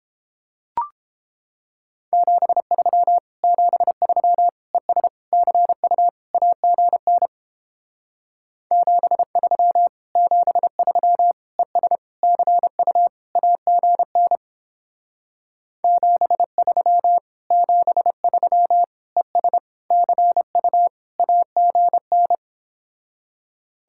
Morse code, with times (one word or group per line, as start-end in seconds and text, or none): 2.1-3.2 73
3.4-4.5 73
4.7-5.1 ES
5.3-6.1 CU
6.3-7.3 AGN
8.7-9.9 73
10.2-11.3 73
11.6-12.0 ES
12.2-13.1 CU
13.4-14.4 AGN
15.8-17.2 73
17.5-18.8 73
19.2-19.6 ES
19.9-20.9 CU
21.2-22.4 AGN